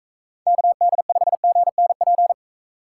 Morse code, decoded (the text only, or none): KD5GNP